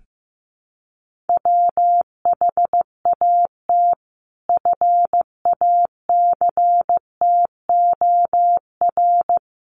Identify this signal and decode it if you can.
Morse code, heard as a keyed tone, about 15 wpm, 700 Hz. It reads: WHAT FACTOR
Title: Sentences from Top 1300 Words 15wpm